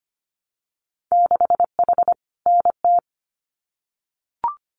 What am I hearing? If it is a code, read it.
Morse code, heard as a keyed tone, about 25 wpm, 700 Hz. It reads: 6H NT
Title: Winter Field Day Exchanges Encore 25wpm